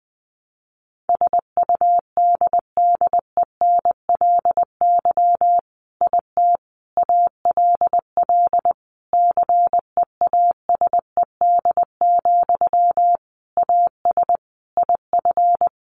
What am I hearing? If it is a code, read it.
Morse code, heard as a keyed tone, about 19 wpm, 700 Hz. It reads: SUDDENLY IT ALL CEASED, AS IF
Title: A Sound Of Thunder 20wpm